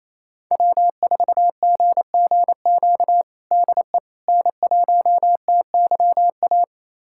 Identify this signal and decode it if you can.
Morse code, heard as a keyed tone, about 27 wpm, 700 Hz. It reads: W4GGQ DE N1TYA
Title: USA Callsign DE Callsign - 5 Characters - 28wpm